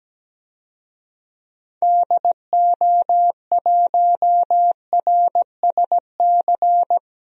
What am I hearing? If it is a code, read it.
Morse code, heard as a keyed tone, about 17 wpm, 700 Hz. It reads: DO1RSC
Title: DXCC-100 Call Signs - 6 Characters - Repeated 2x - 17wpm